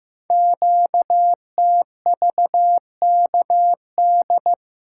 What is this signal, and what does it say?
Morse code, heard as a keyed tone, about 15 wpm, 700 Hz. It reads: QTVKD
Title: Code Groups - 5 Letter:Numbers - 15wpm